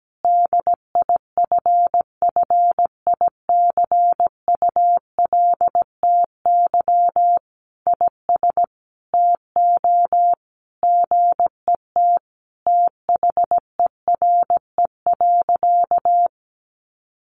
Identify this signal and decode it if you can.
Morse code, heard as a keyed tone, about 17 wpm, 700 Hz. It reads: DIFFICULTY IS TO GET THERE.
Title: Sentences from Top 1500 Words 17wpm